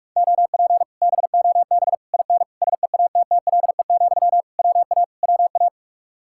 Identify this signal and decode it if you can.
Morse code, heard as a keyed tone, about 46 wpm, 700 Hz. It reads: OP BOB IN SEATTLE, WA WA